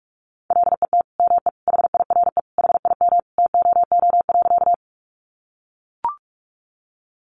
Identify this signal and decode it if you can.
Morse code, heard as a keyed tone, about 44 wpm, 700 Hz. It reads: LET ME HIRE HIM TOO.
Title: Sentences from Top 1400 Words 45wpm